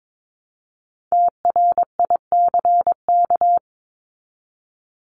Morse code, heard as a keyed tone, about 22 wpm, 700 Hz. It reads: TRICK
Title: Top 1251-1500 Words 22wpm